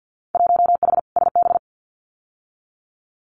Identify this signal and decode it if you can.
Morse code, heard as a keyed tone, about 50 wpm, 700 Hz. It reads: 1H SB
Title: Winter Field Day Exchanges 50wpm